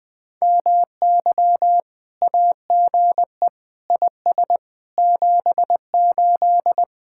Morse code, heard as a keyed tone, about 20 wpm, 700 Hz. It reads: MY AGE IS 78
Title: Speed-Racing Common QSO Phrases - 20wpm